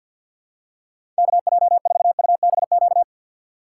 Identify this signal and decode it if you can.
Morse code, heard as a keyed tone, about 47 wpm, 700 Hz. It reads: KJ4UBQ
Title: USA Call Signs - 6x Character Spacing - 50wpm